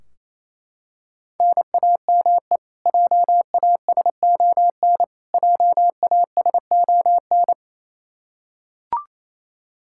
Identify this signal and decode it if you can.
Morse code, heard as a keyed tone, about 27 wpm, 700 Hz. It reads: NAME JASON JASON
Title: K1USN SST QSO Practice 28wpm